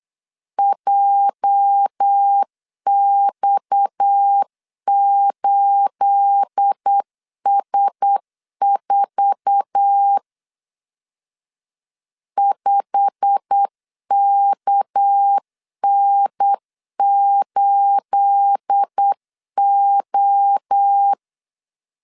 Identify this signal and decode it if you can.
Morse code, heard as a keyed tone, about 8 wpm, 800 Hz. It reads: JX8S4 5KN8O
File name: LEZIONI / 9 28 8wpm 3512